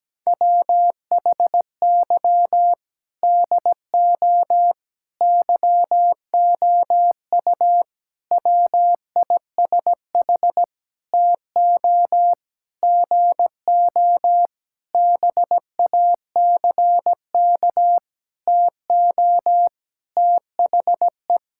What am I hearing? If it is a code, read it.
Morse code, heard as a keyed tone, about 17 wpm, 700 Hz. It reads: WHY DO YOU WISH TO GO BACK TO THE
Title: Sentences from Top 1600 Words 17wpm